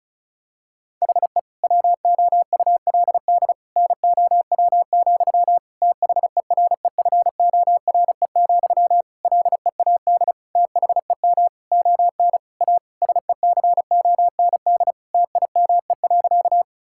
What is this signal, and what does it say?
Morse code, heard as a keyed tone, about 34 wpm, 700 Hz. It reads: HE WOULD NOW, THEREFORE, LEAD THEM ON A SECOND TIME.
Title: Sentences from Top 1000 Words - Review Part 5 - 35wpm